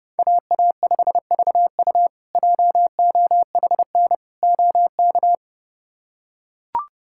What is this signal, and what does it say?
Morse code, heard as a keyed tone, about 32 wpm, 700 Hz. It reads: AA5VU JOHN OK
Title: K1USN SST QSO Practice - 30wpm